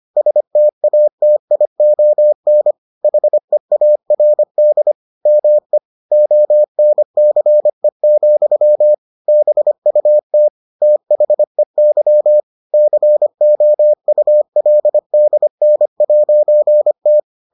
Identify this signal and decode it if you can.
Morse code, heard as a keyed tone, about 25 wpm, 600 Hz. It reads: STATION HEARD ME ONCE, BUT THEY COULDN'T